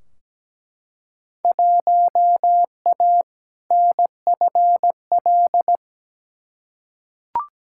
Morse code, heard as a keyed tone, about 17 wpm, 700 Hz. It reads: 1A NFL
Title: Field Day Exchanges Encore - 17wpm